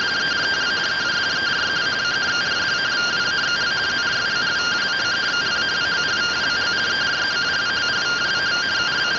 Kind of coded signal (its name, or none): two-tone teleprinter signal